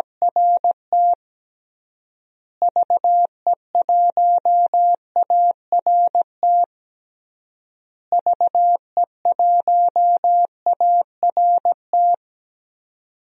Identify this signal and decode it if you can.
Morse code, heard as a keyed tone, about 17 wpm, 700 Hz. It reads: RT VE1ART VE1ART
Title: DXCC-100 Call Signs - 6 Characters - Repeated 3x - 17wpm